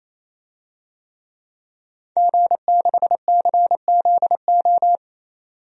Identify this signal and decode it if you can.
Morse code, heard as a keyed tone, about 27 wpm, 700 Hz. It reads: G6CZO
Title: DXCC-100 Call Signs - 5 Characters - 28wpm